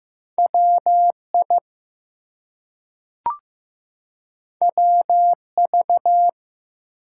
Morse code, heard as a keyed tone, about 15 wpm, 700 Hz. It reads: WI WV
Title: US States and Canadian Provinces 15wpm